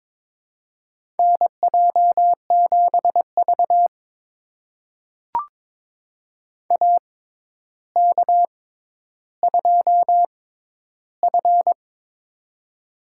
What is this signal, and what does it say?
Morse code, heard as a keyed tone, about 22 wpm, spaced as for 11 wpm, 700 Hz. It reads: NJ7V AK2F
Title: USA Call Signs - 4 Characters - 6x Character Spacing - 22wpm